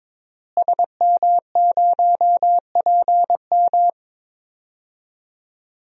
Morse code, heard as a keyed tone, about 22 wpm, 700 Hz. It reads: SM0PM
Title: DXCC-100 Call Signs - 5 Characters - Repeated 2x - 22wpm